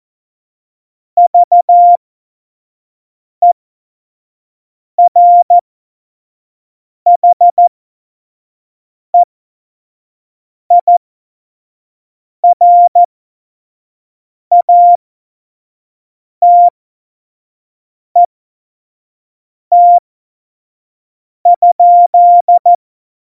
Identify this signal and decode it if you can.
Morse code, heard as a keyed tone, about 14 wpm, spaced as for 5 wpm, 700 Hz. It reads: VERHEIRATET?
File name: 25 F70